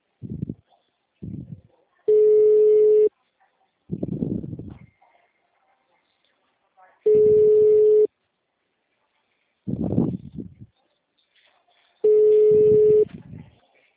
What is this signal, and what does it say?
Ringback tone (Europe, 425 Hz), with speech.